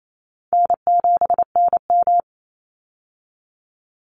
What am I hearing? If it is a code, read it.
Morse code, heard as a keyed tone, about 28 wpm, 700 Hz. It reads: N7NM
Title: Easy Call Signs - 28wpm